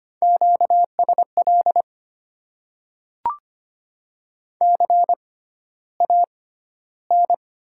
Morse code, heard as a keyed tone, about 25 wpm, spaced as for 13 wpm, 700 Hz. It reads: QSL CAN
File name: Top 90 QSO Elements - 6x Character Spacing - 25wpm